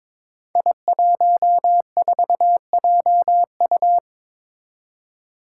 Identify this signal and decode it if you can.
Morse code, heard as a keyed tone, about 22 wpm, 700 Hz. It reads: I14JU
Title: Five Challenging Letter-Number Combinations 22wpm